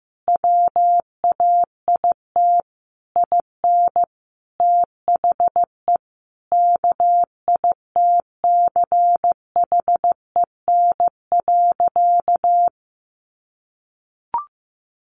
Morse code, heard as a keyed tone, about 15 wpm, 700 Hz. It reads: WAIT IN THE KITCHEN.
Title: Sentences from Top 1400 Words 15wpm